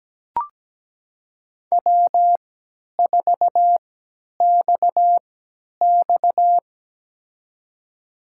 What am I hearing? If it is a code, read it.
Morse code, heard as a keyed tone, about 17 wpm, spaced as for 10 wpm, 700 Hz. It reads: W4XX
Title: USA Call Signs - 4 Characters - 3x Character Spacing - 17wpm